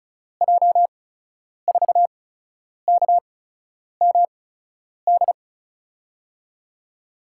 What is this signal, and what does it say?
Morse code, heard as a keyed tone, about 36 wpm, spaced as for 10 wpm, 700 Hz. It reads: J4KMD
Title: USA Call Signs - 6 Characters - 8x Character Spacing - 35wpm